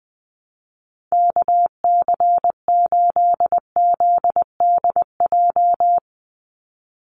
Morse code, heard as a keyed tone, about 20 wpm, 700 Hz. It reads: KC8ZDJ
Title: USA Call Signs 20wpm